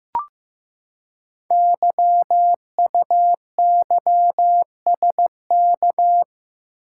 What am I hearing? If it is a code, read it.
Morse code, heard as a keyed tone, about 15 wpm, 700 Hz. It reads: YUYSK
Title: Code Groups - 5 Letters - 15wpm